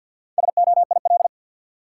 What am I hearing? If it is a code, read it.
Morse code, heard as a keyed tone, about 52 wpm, 700 Hz. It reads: SOIL
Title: Sentences from Top 1500 Words - New Word Drill Alt - 50wpm